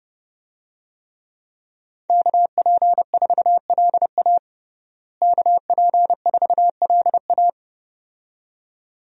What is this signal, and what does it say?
Morse code, heard as a keyed tone, about 30 wpm, 700 Hz. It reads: KP4LA KP4LA
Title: USA Call Signs - 5 Characters - Repeated 2x - 30wpm